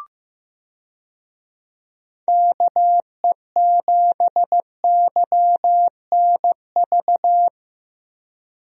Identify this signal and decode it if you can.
Morse code, heard as a keyed tone, about 15 wpm, 700 Hz. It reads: KE7YNV
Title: USA Call Signs - 6 Characters - Repeated 3x - 15wpm